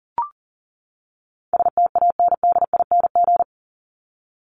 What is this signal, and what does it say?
Morse code, heard as a keyed tone, about 40 wpm, 700 Hz. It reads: STANDING
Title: ING Suffix - Top 100 Words with Base Words 40wpm